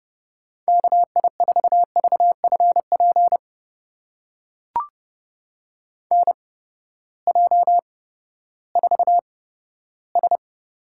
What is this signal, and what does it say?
Morse code, heard as a keyed tone, about 32 wpm, spaced as for 15 wpm, 700 Hz. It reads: KI4VFP NJ4S
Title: USA Call Signs - 8x Character Spacing - 30wpm